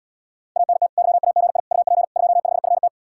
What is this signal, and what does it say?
Morse code, heard as a keyed tone, about 80 wpm, 700 Hz. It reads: ANT YAGI UP 25FT